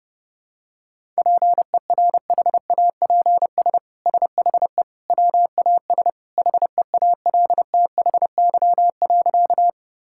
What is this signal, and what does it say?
Morse code, heard as a keyed tone, about 32 wpm, 700 Hz. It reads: PERHAPS SHE WAS HEALTHY.
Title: Sentences from Top 1000 Words - Review Part 5 - 30wpm